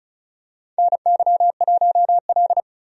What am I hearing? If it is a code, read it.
Morse code, heard as a keyed tone, about 34 wpm, 700 Hz. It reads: NY1L
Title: USA Call Signs - 4 Characters - 35wpm